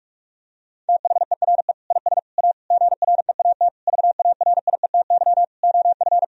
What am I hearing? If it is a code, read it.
Morse code, heard as a keyed tone, about 41 wpm, 700 Hz. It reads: THERE IS A GREAT VARIETY OF
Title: Sentences from Top 1000 Words - Review Part 4 - 45wpm